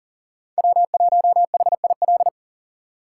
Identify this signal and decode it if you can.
Morse code, heard as a keyed tone, about 40 wpm, 700 Hz. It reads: W1HIL
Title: USA Call Signs - 5 Characters - 3x Character Spacing - 40wpm